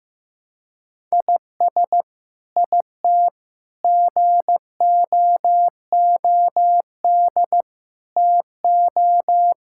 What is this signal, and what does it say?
Morse code, heard as a keyed tone, about 15 wpm, 700 Hz. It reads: IS IT GOOD TO
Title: Sentences from Top 1000 Words - Review Part 6 - 15wpm